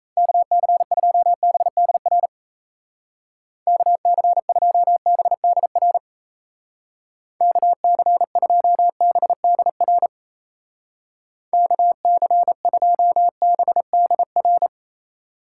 Morse code, heard as a keyed tone, about 34 wpm, 700 Hz. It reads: KC2BDR KC2BDR KC2BDR KC2BDR
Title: Speed Racing - USA 2x3 Call Signs 28wpm